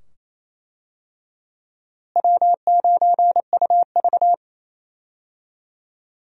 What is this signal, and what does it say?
Morse code, heard as a keyed tone, about 27 wpm, 700 Hz. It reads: W9UV
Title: USA Call Signs - 4 Characters - Repeated 2x - 28wpm